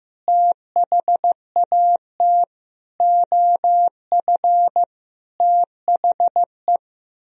Morse code, heard as a keyed tone, about 15 wpm, 700 Hz. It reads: THAT OF THE
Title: Sentences from Top 1000 Words - Review Part 2 - 15wpm